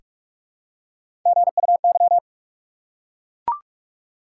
Morse code, heard as a keyed tone, about 43 wpm, 700 Hz. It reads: GUY
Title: Three Letter Words - 45wpm